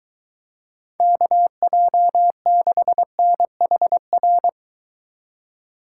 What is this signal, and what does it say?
Morse code, heard as a keyed tone, about 23 wpm, 700 Hz. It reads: KJ6NHR